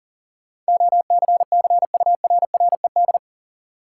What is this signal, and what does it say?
Morse code, heard as a keyed tone, about 39 wpm, 700 Hz. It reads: OCCURRED
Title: Sets of 5 Words Encore - New Word Drill - 40wpm